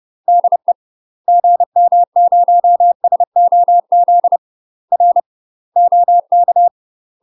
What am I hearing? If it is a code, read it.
Morse code, heard as a keyed tone, about 30 wpm, 700 Hz. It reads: DE GM0SOZ R OK
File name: UK MASTER QSOs TRAINING / 30WPM / UK MASTER QSOs 0012